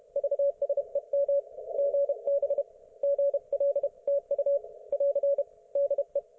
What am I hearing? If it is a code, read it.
Morse code, heard as a keyed tone, about 30 wpm, 560 Hz. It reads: VSEM GB GLTU <AR> DE